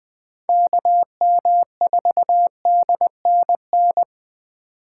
Morse code, heard as a keyed tone, about 20 wpm, 700 Hz. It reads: KM4DNN
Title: Speed Racing - USA Call Signs 17wpm